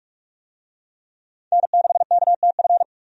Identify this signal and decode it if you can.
Morse code, heard as a keyed tone, about 46 wpm, 700 Hz. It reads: N6KTF
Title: USA Call Signs - 5 Characters - 45wpm